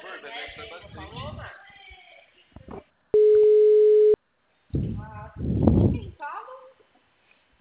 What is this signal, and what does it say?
Ringback tone (Europe, 425 Hz), with speech.